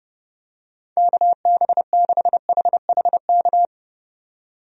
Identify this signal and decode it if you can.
Morse code, heard as a keyed tone, about 30 wpm, 700 Hz. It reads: KB6HHK
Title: USA Call Signs - 6 Characters - 30wpm